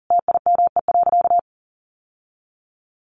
Morse code, heard as a keyed tone, about 40 wpm, 700 Hz. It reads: TIME.
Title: Sentences from Top 1000 Words 40wpm